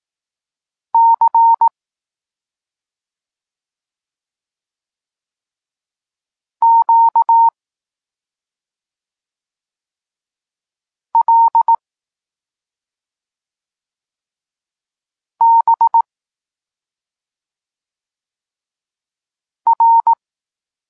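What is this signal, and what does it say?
Morse code, heard as a keyed tone, about 18 wpm, 920 Hz. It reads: C Q L B R
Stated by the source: Polish Morse lesson teaching the letter C, from SP9MOA Niepołomicki Klub Krótkofalowców.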